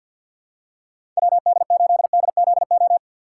Morse code, heard as a keyed tone, about 53 wpm, 700 Hz. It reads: WD8DZO
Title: USA Call Signs - 4x Character Spacing - 50wpm